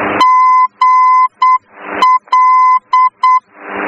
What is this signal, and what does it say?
Morse code, heard as a keyed tone, about 8 wpm, 1020 Hz. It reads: GL
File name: germany / mx / GL-D